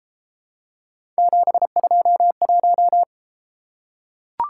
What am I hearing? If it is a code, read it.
Morse code, heard as a keyed tone, about 30 wpm, 700 Hz. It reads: 721